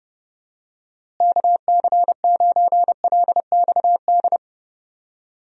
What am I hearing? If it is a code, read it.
Morse code, heard as a keyed tone, about 29 wpm, 700 Hz. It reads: KC9LXD
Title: USA Call Signs - 6 Characters - Repeated 3x - 30wpm